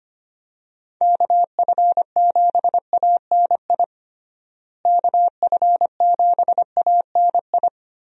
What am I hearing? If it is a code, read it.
Morse code, heard as a keyed tone, about 25 wpm, 700 Hz. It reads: KF7ANI KF7ANI
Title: USA Call Signs - 6 Characters - Repeated 2x - 25wpm